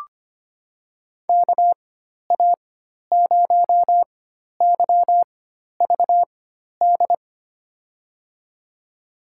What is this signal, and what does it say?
Morse code, heard as a keyed tone, about 25 wpm, spaced as for 12 wpm, 700 Hz. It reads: KA0YVD
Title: USA Call Signs - 6 Characters - 4x Character Spacing - 25wpm